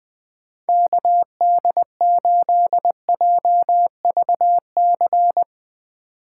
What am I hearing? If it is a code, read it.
Morse code, heard as a keyed tone, about 20 wpm, 700 Hz. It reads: KD8JVC
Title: Speed Racing - USA Call Signs 20wpm